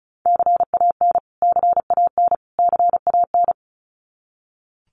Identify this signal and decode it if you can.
Morse code, heard as a keyed tone, about 34 wpm, 700 Hz. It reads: CAN CAN CAN
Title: Top 200 Words Repeated 35wpm